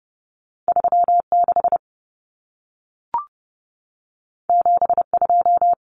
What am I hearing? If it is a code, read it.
Morse code, heard as a keyed tone, about 30 wpm, 700 Hz. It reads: 36 72
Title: Common Cut Numbers - 2-Digit 30wpm